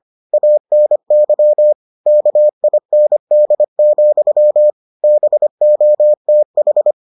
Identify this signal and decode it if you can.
Morse code, heard as a keyed tone, about 25 wpm, 600 Hz. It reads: ANY KIND, BOTH